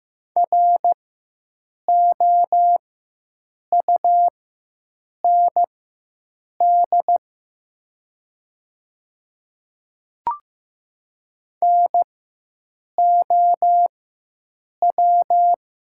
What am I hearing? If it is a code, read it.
Morse code, heard as a keyed tone, about 15 wpm, spaced as for 7 wpm, 700 Hz. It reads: ROUND NOW
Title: 4x Character Spacing - Top 200 Words - 15wpm